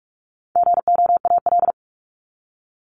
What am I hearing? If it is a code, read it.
Morse code, heard as a keyed tone, about 44 wpm, 700 Hz. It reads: GOAL
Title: Top 751-1000 Words 45wpm